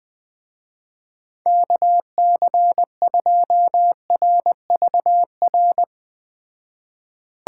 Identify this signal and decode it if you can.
Morse code, heard as a keyed tone, about 21 wpm, 700 Hz. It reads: KC2RVR